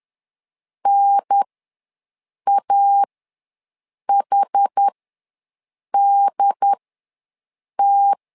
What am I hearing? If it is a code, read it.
Morse code, heard as a keyed tone, about 10 wpm, spaced as for 6 wpm, 800 Hz. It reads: NAHDT
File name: LEZIONI / 4 Q1S ws 10wpm 2208